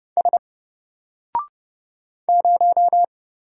Morse code, heard as a keyed tone, about 29 wpm, 700 Hz. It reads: S 0
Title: Single Letter-Number 30wpm